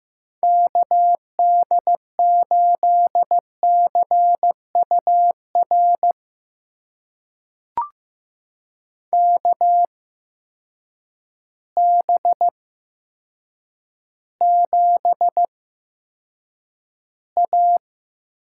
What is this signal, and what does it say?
Morse code, heard as a keyed tone, about 15 wpm, spaced as for 8 wpm, 700 Hz. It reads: KD8CUR KB7A
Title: USA Call Signs - 6 Characters - 8x Character Spacing - 15wpm